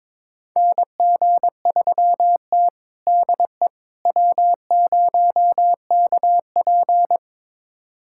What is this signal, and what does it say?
Morse code, heard as a keyed tone, about 22 wpm, 700 Hz. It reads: NG3T DE W0KP
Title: USA Callsign DE Callsign - 4 Characters - 22wpm